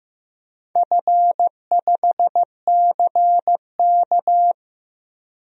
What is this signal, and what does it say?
Morse code, heard as a keyed tone, about 15 wpm, 700 Hz. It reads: F5CK
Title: DXCC-100 Call Signs - 4 Characters - Repeated 3x - 15wpm